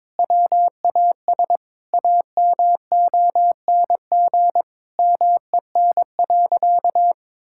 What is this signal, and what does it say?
Morse code, heard as a keyed tone, about 22 wpm, 700 Hz. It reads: WAS AMONG MEN.
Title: Sentences from Top 1600 Words 22wpm